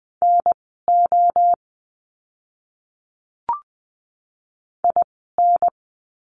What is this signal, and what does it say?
Morse code, heard as a keyed tone, about 20 wpm, spaced as for 14 wpm, 700 Hz. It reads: NO IN
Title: 030 Lesson 6f - Letter i - Words 20wpm